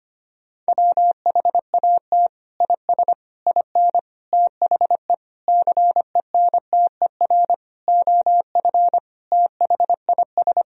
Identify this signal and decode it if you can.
Morse code, heard as a keyed tone, about 25 wpm, 700 Hz. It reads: WHAT IS IN THE CENTER OF THIS
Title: Sentences from Top 1000 Words - Review Part 2 - 25wpm